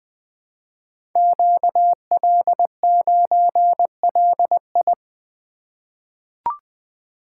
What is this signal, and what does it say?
Morse code, heard as a keyed tone, about 20 wpm, 700 Hz. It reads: QL9LI
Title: Code Groups - 5 Letter:Numbers - 20wpm